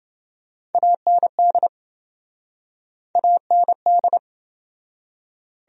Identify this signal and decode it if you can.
Morse code, heard as a keyed tone, about 28 wpm, 700 Hz. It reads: AND AND